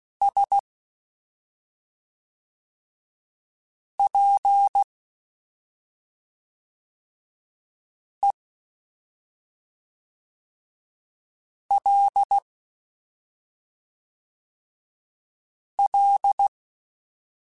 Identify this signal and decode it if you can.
Morse code, heard as a keyed tone, about 16 wpm, 800 Hz. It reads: S P E L L
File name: Special INGs / Special INGs-mp3s / SlowToFaster ing @5